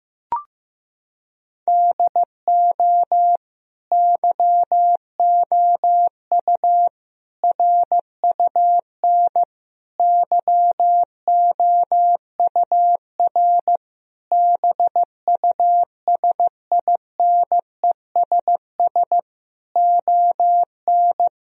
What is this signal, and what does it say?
Morse code, heard as a keyed tone, about 15 wpm, 700 Hz. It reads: DO YOU RUN YOUR BUSINESS ON
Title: Sentences from Top 1000 Words - Review Part 4 - 15wpm